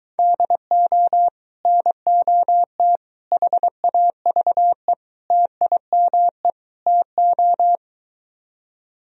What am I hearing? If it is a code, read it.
Morse code, heard as a keyed tone, about 23 wpm, 700 Hz. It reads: DO NOT HAVE TIME TO